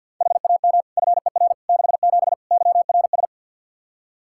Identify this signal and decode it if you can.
Morse code, heard as a keyed tone, about 53 wpm, 700 Hz. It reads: HAM FER 67 YRS
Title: Practice QSO - Exchange 3 - More Info - 50wpm